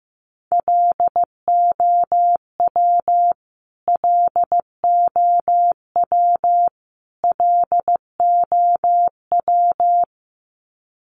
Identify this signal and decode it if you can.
Morse code, heard as a keyed tone, about 15 wpm, 700 Hz. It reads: LOW LOW LOW
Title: Top 500 Words Repeated 15wpm